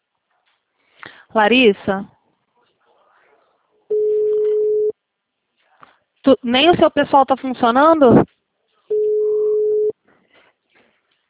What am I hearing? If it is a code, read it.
Ringback tone (Europe, 425 Hz), with speech.